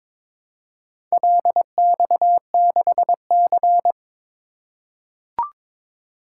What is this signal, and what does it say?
Morse code, heard as a keyed tone, about 22 wpm, 700 Hz. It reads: LX6C
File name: practice / 22wpm / Four Letter-Number Combinations - 22wpm